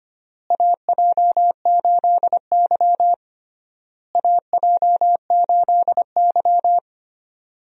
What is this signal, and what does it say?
Morse code, heard as a keyed tone, about 25 wpm, 700 Hz. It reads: AJ8Y AJ8Y
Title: USA Call Signs - 4 Characters - Repeated 3x - 25wpm